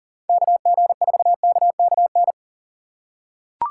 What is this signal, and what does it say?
Morse code, heard as a keyed tone, about 40 wpm, 700 Hz. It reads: KG4KKN
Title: USA Call Signs - 6 Characters - 3x Character Spacing - 40wpm